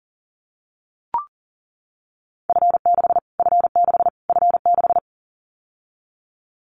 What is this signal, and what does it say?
Morse code, heard as a keyed tone, about 40 wpm, 700 Hz. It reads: F6 F6 F6
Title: Callsign Prefixes for Top 25 DXCC - Repeated 3x - 40wpm